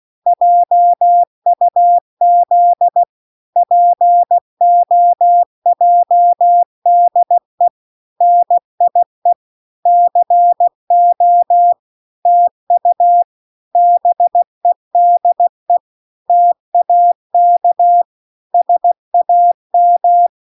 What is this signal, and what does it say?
Morse code, heard as a keyed tone, about 16 wpm, 700 Hz. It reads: JUZ POJDE NIE CO TU BEDE TAK SAM